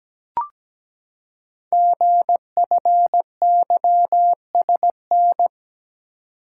Morse code, heard as a keyed tone, about 17 wpm, 700 Hz. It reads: GFYSN
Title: Code Groups - 5 Letter:Numbers - 17wpm